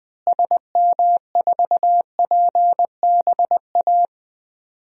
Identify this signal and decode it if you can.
Morse code, heard as a keyed tone, about 20 wpm, 700 Hz. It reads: SM4PBA
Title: DXCC-100 Call Signs - 6 Characters - Repeated 3x - 20wpm